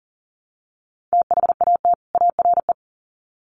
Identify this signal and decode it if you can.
Morse code, heard as a keyed tone, about 40 wpm, 700 Hz. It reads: THAT ARE
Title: Sets of 2 Words 40wpm